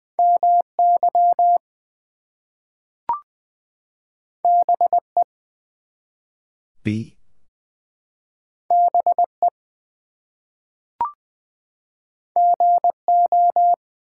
Morse code, heard as a keyed tone, about 20 wpm, 700 Hz. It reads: MY BE BE GO
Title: Two Letter Words - 20wpm